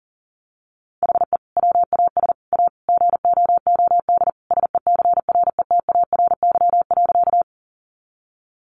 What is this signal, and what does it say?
Morse code, heard as a keyed tone, about 40 wpm, 700 Hz. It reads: HE WAS A GOOD SECRETARY.
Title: Sentences from Top 1300 Words 40wpm